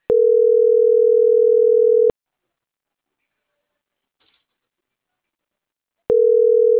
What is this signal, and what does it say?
Ringback tone (North America), 440 + 480 Hz.